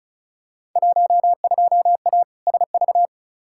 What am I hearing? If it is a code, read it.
Morse code, heard as a keyed tone, about 36 wpm, 700 Hz. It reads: 12A SV
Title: Field Day Exchanges Encore - 35wpm